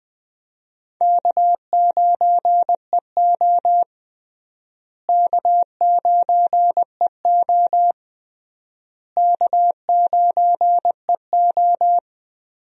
Morse code, heard as a keyed tone, about 19 wpm, 700 Hz. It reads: K9EO K9EO K9EO
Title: USA Call Signs - 4 Characters - Repeated 3x - 20wpm